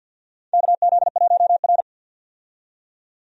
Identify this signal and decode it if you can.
Morse code, heard as a keyed tone, about 49 wpm, 700 Hz. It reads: KZ1R
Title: USA Call Signs - 4 Characters - 50wpm